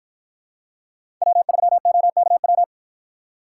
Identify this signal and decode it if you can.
Morse code, heard as a keyed tone, about 53 wpm, 700 Hz. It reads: W3OKW